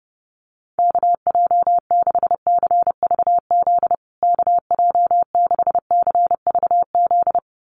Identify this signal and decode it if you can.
Morse code, heard as a keyed tone, about 30 wpm, 700 Hz. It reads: KJ6CVZ KJ6CVZ
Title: USA Call Signs Repeated 2x 30wpm